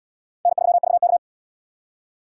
Morse code, heard as a keyed tone, about 77 wpm, 700 Hz. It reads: N3VK